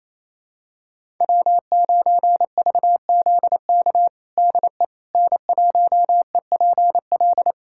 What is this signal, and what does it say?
Morse code, heard as a keyed tone, about 28 wpm, 700 Hz. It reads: W9VZK DE N1EPL
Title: USA Callsign DE Callsign - 5 Characters - 28wpm